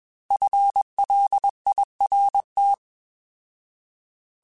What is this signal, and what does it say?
Morse code, heard as a keyed tone, about 21 wpm, 800 Hz. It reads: FLIRT
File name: Alphabetical / Alphabetical mp3s / Fs @20